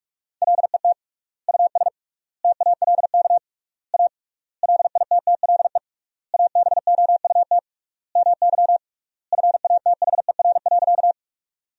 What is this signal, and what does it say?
Morse code, heard as a keyed tone, about 46 wpm, 700 Hz. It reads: LET US TALK A LITTLE ABOUT MY FATHER.
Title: 3x Word-Spacing - Sentences from Top 400 Words - 45wpm